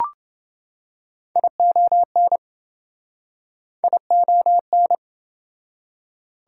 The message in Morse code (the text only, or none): ION ION